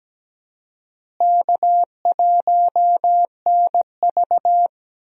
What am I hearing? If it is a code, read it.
Morse code, heard as a keyed tone, about 17 wpm, 700 Hz. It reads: K1NV
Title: USA Call Signs - 4 Characters - Repeated 3x - 17wpm